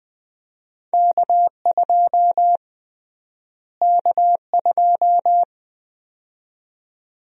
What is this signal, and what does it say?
Morse code, heard as a keyed tone, about 20 wpm, 700 Hz. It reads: K2 K2
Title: USA Call Sign Prefixes - Repeated 2x - 20wpm